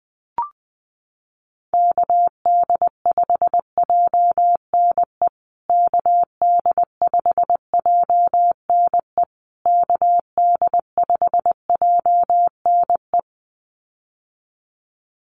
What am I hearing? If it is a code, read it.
Morse code, heard as a keyed tone, about 20 wpm, 700 Hz. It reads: KD5JNE KD5JNE KD5JNE
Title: USA Call Signs Repeated 3x 20wpm